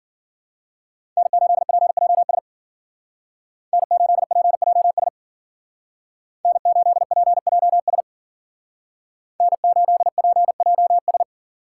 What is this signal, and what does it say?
Morse code, heard as a keyed tone, about 52 wpm, 700 Hz. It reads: N8PJS N8PJS N8PJS N8PJS